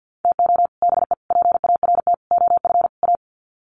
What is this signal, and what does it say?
Morse code, heard as a keyed tone, about 50 wpm, 700 Hz. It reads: TO BE PART OF A
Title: Sets of 5 Words Encore2 50wpm